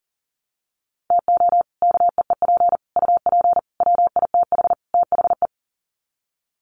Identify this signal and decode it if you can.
Morse code, heard as a keyed tone, about 40 wpm, 700 Hz. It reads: TO KEEP UP WITH THE
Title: Sets of 5 Words Encore 40wpm